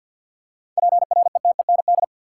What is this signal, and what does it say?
Morse code, heard as a keyed tone, about 48 wpm, 700 Hz. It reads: PRETEND
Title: Seven Letter Words - 50wpm